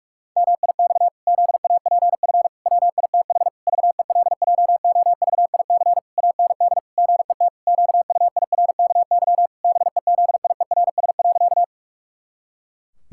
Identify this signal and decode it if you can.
Morse code, heard as a keyed tone, about 48 wpm, 700 Hz. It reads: MIX ZAPF WITH VELJOVIC AND GET QUIRKY BEZIERS.
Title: Pangram Challenge v2 - 45wpm